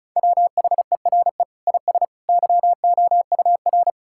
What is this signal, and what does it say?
Morse code, heard as a keyed tone, about 34 wpm, 700 Hz. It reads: WHERE IS YOUR
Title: Sentences from Top 1000 Words - Review Part 5 - 35wpm